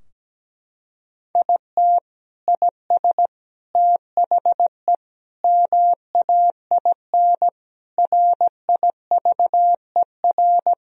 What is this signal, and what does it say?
Morse code, heard as a keyed tone, about 17 wpm, 700 Hz. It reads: IT IS THE MAIN RIVER
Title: Sentences from Top 1000 Words - Review Part 2 - 17wpm